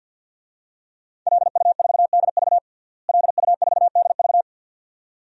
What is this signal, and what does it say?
Morse code, heard as a keyed tone, about 51 wpm, 700 Hz. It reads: LU4DV LU4DV
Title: DXCC-100 Call Signs - 5 Characters - Repeated 2x - 50wpm